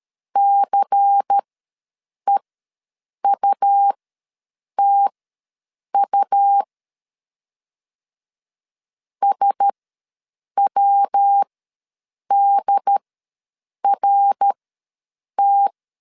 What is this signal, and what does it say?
Morse code, heard as a keyed tone, about 13 wpm, spaced as for 7 wpm, 800 Hz. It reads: CEUTU SWDRT
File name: LEZIONI / 5 LBVC / 5 LBVC ws 12wpm 3159